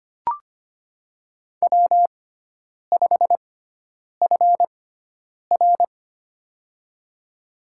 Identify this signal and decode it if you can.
Morse code, heard as a keyed tone, about 25 wpm, spaced as for 9 wpm, 700 Hz. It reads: W5FR
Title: USA Call Signs - 6x Character Spacing - 25wpm